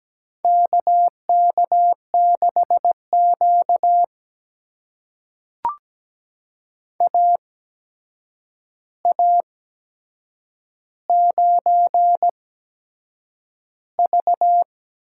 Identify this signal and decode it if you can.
Morse code, heard as a keyed tone, about 17 wpm, spaced as for 7 wpm, 700 Hz. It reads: KK6Q AA9V
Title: USA Call Signs - 4 Characters - 8x Character Spacing - 17wpm